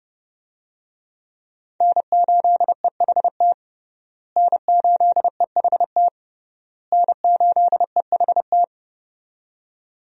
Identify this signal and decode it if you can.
Morse code, heard as a keyed tone, about 30 wpm, 700 Hz. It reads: N8EHT N8EHT N8EHT
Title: USA Call Signs - 5 Characters - Repeated 3x - 30wpm